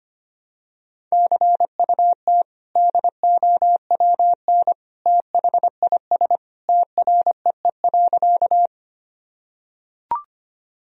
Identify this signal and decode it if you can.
Morse code, heard as a keyed tone, about 25 wpm, 700 Hz. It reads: CUT DOWN THIS TREE.
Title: Sentences from Top 1000 Words - Review Part 1 - 25wpm